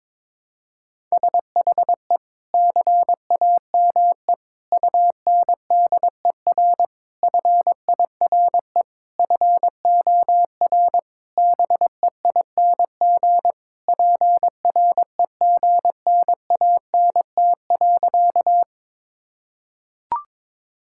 Morse code, heard as a keyed tone, about 22 wpm, 700 Hz. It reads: SHE CAME UNDER FIRE FOR BEING PREGNANT.
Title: Sentences from Top 1000 Words - Review Part 6 - 22wpm